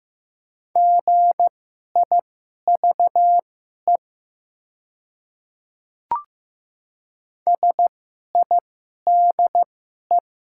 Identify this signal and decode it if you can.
Morse code, heard as a keyed tone, about 15 wpm, spaced as for 11 wpm, 700 Hz. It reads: GIVE SIDE
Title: 2x Character Spacing - Top 200 Words - 15wpm